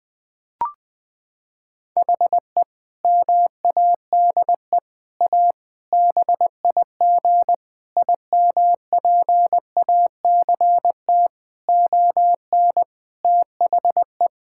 Morse code, heard as a keyed tone, about 20 wpm, 700 Hz. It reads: HE MADE A BIG IMPACT ON THE
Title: Sentences from Top 1000 Words - Review Part 6 - 20wpm